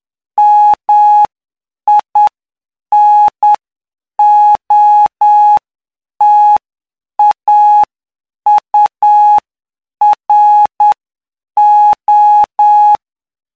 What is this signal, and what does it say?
Morse code, heard as a keyed tone, about 9 wpm, spaced as for 7 wpm, 840 Hz. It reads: MINOTAURO
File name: cw / py2lcd / 11-PY2LCD-Li��o 20